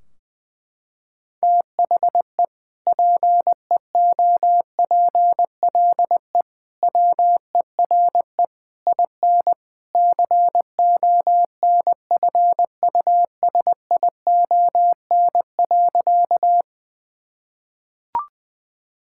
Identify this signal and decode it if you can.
Morse code, heard as a keyed tone, about 20 wpm, 700 Hz. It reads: THE PEOPLE WERE IN CONFUSION.
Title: Sentences from Top 1600 Words 20wpm